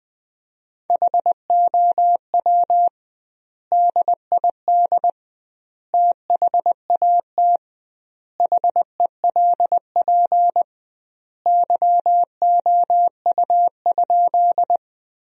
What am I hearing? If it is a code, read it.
Morse code, heard as a keyed tone, about 20 wpm, 700 Hz. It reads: HOW DID THAT HELP YOU?
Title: 2x Word-Spacing - Sentences from Top 200 Words - 20wpm